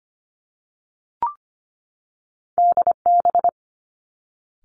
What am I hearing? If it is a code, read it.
Morse code, heard as a keyed tone, about 25 wpm, 700 Hz. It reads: DB
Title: Two Challenging Letter-Number Combinations 25wpm